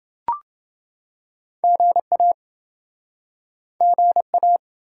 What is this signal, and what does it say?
Morse code, heard as a keyed tone, about 29 wpm, 700 Hz. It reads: GA GA